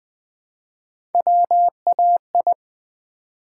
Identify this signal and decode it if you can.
Morse code, heard as a keyed tone, about 20 wpm, 700 Hz. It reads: WAI
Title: Build A Word - 1 Letter At A Time - 3 to 5-Letter Words - 20wpm